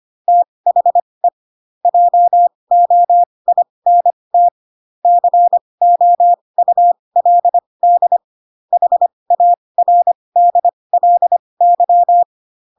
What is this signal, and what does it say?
Morse code, heard as a keyed tone, about 25 wpm, spaced as for 20 wpm, 700 Hz. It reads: THE JOINT COULD HARDLY
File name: Pinocchio / Pinocchio ch8 15 25 ik1hge